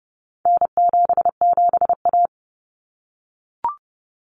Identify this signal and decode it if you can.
Morse code, heard as a keyed tone, about 30 wpm, 700 Hz. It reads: N77A
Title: Common Cut Numbers - 4-Digit 30wpm